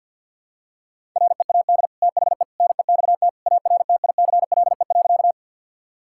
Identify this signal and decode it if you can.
Morse code, heard as a keyed tone, about 52 wpm, 700 Hz. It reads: READ THE NEXT ARTICLE.
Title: Sentences from Top 1000 Words - Review Part 5 - 50wpm